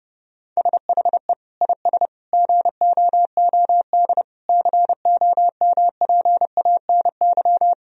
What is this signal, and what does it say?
Morse code, heard as a keyed tone, about 32 wpm, 700 Hz. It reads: SHE IS GOOD COMPANY
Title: Sentences from Top 1000 Words - Review Part 3 - 30wpm